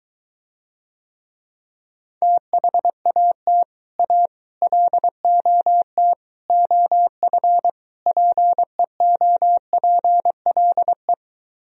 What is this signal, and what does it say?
Morse code, heard as a keyed tone, about 23 wpm, 700 Hz. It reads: THAT A LOT OF PEOPLE